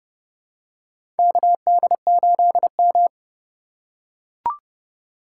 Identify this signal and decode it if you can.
Morse code, heard as a keyed tone, about 29 wpm, 700 Hz. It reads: KD8M
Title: USA Call Signs - 4 Characters - 2x Character Spacing - 30wpm